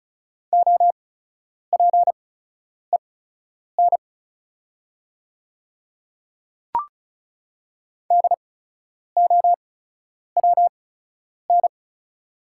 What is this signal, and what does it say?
Morse code, heard as a keyed tone, about 34 wpm, spaced as for 10 wpm, 700 Hz. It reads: OPEN DOWN
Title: 8x Character Spacing - Top 300 Words - 35wpm